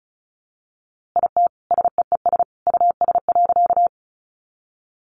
Morse code, heard as a keyed tone, about 34 wpm, 700 Hz. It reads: IT SEES US.
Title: A Sound Of Thunder 35wpm